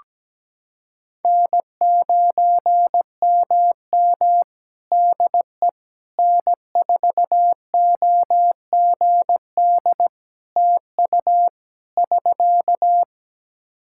Morse code, heard as a keyed tone, about 17 wpm, 700 Hz. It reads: N9MM DE N4OGD TU <SK>
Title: K1USN SST QSO Practice 17wpm